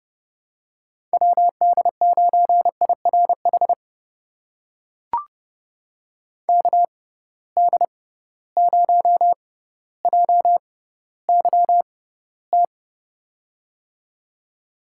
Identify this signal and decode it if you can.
Morse code, heard as a keyed tone, about 30 wpm, spaced as for 15 wpm, 700 Hz. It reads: WD9IRH KD0JYT
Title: USA Call Signs - 6 Characters - 6x Character Spacing - 30wpm